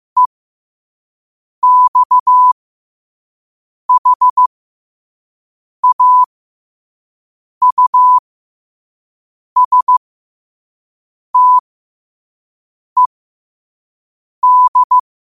Morse code, heard as a keyed tone, about 15 wpm, spaced as for 5 wpm, 1000 Hz. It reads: EXHAUSTED
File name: qotdfiles / 5wpm